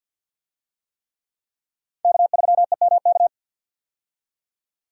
Morse code, heard as a keyed tone, about 46 wpm, 700 Hz. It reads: K3EMK
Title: USA Call Signs - 5 Characters - 50wpm